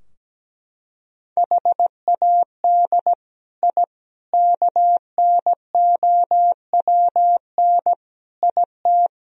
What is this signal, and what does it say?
Morse code, heard as a keyed tone, about 17 wpm, 700 Hz. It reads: HAD I KNOWN IT
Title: Sentences from Top 1000 Words - Review Part 3 - 17wpm